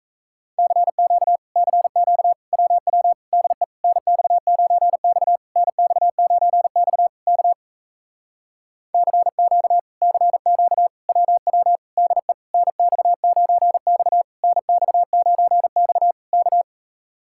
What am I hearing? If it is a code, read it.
Morse code, heard as a keyed tone, about 38 wpm, 700 Hz. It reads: CQ CQ WW DE NX9X NX9X K CQ CQ WW DE NX9X NX9X K